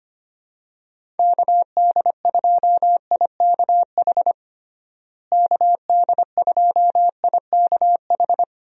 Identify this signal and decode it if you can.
Morse code, heard as a keyed tone, about 25 wpm, 700 Hz. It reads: KD2IKH KD2IKH
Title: USA Call Signs - 6 Characters - Repeated 2x - 25wpm